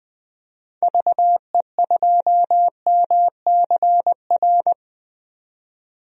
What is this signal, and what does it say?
Morse code, heard as a keyed tone, about 20 wpm, 700 Hz. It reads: VE2MCR
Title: DXCC-100 Call Signs - 6 Characters - 20wpm